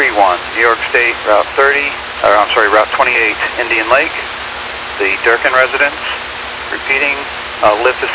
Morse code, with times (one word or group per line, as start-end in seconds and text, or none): none